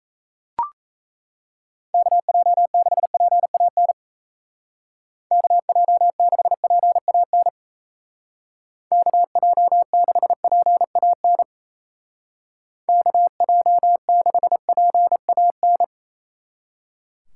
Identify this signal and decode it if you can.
Morse code, heard as a keyed tone, about 35 wpm, 700 Hz. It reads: KJ6PAN KJ6PAN KJ6PAN KJ6PAN